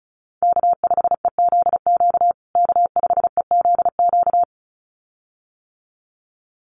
Morse code, heard as a keyed tone, about 34 wpm, 700 Hz. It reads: K5EZQ K5EZQ
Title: USA Call Signs Repeated 2x 35wpm